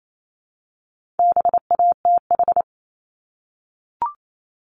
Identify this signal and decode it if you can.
Morse code, heard as a keyed tone, about 28 wpm, 700 Hz. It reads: BATH
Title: Multi-letter Phonemes Repeated 3x 28wpm